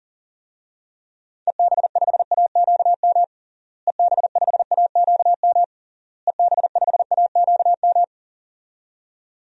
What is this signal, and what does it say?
Morse code, heard as a keyed tone, about 40 wpm, 700 Hz. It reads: EB5AQM EB5AQM EB5AQM
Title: DXCC-100 Call Signs - 6 Characters - Repeated 3x - 40wpm